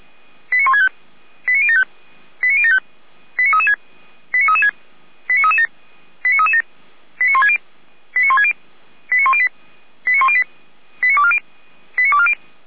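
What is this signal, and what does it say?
Five-tone selective call (ZVEI), heard as quick runs of five tones, tones of about 70 ms. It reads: 89165 89075 89075 89336 89336 89337 89337 89160 89160 89108 89108 89240 89240
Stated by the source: Received on an SDR, ZVEI five-tone code 89072.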